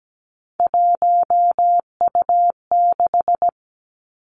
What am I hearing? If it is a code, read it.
Morse code, heard as a keyed tone, about 17 wpm, 700 Hz. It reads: 1U6
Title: Three Challenging Letter-Number Combinations 17wpm